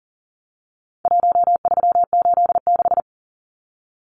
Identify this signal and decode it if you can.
Morse code, heard as a keyed tone, about 40 wpm, 700 Hz. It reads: 1386
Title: Numbers - Contest Style - 4-Digit 40wpm